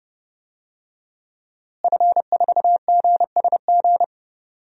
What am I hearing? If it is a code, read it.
Morse code, heard as a keyed tone, about 30 wpm, 700 Hz. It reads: F4GSG
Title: DXCC-100 Call Signs - 5 Characters - Repeated 3x - 30wpm